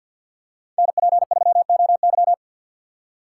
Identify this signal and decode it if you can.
Morse code, heard as a keyed tone, about 52 wpm, 700 Hz. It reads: NP3OY
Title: USA Call Signs - 2x Character Spacing - 50wpm